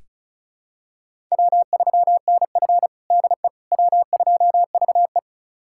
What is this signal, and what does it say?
Morse code, heard as a keyed tone, about 36 wpm, 700 Hz. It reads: W3NF DE W2VE
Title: USA Callsign DE Callsign - 4 Characters - 35wpm